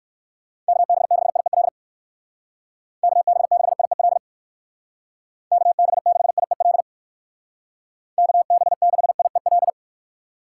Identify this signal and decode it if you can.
Morse code, heard as a keyed tone, about 65 wpm, 700 Hz. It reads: KB6IEL KB6IEL KB6IEL KB6IEL